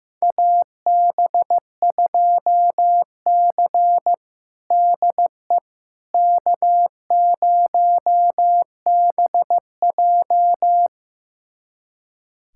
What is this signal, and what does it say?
Morse code, heard as a keyed tone, about 16 wpm, 700 Hz. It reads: AB2C DE K0BJ
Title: USA Callsign DE Callsign - 4 Characters - 15wpm